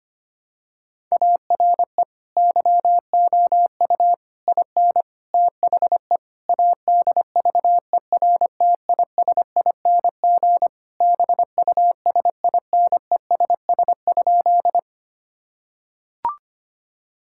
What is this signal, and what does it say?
Morse code, heard as a keyed tone, about 25 wpm, 700 Hz. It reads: ARE YOU IN THE ADVERTISING BUSINESS?
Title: Sentences from Top 1000 Words - Review Part 6 - 25wpm